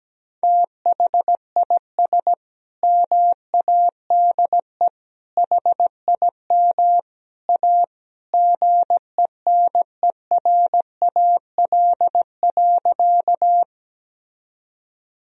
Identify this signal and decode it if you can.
Morse code, heard as a keyed tone, about 17 wpm, 700 Hz. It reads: THIS MADE HIM A GENERAL.
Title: Sentences from Top 1000 Words - Review Part 3 - 17wpm